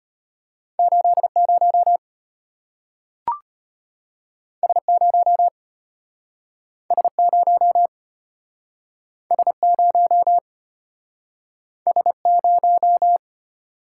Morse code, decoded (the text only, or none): 80 S0 S0 S0 S0